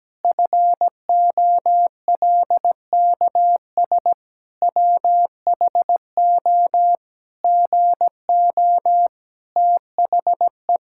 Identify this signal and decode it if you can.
Morse code, heard as a keyed tone, about 17 wpm, 700 Hz. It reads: FOLKS WHO GO THE
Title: Pangram Challenge v2 - 17wpm